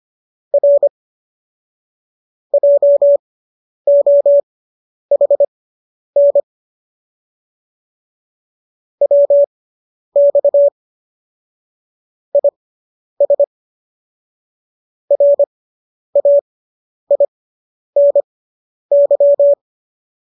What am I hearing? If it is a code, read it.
Morse code, heard as a keyed tone, about 25 wpm, spaced as for 10 wpm, 570 Hz. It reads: R JOHN WX IS RAINY